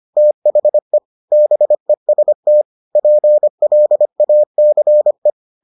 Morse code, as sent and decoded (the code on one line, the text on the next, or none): - .... / -... ... - / .--. .-.. .- -.-. .
THE BEST PLACE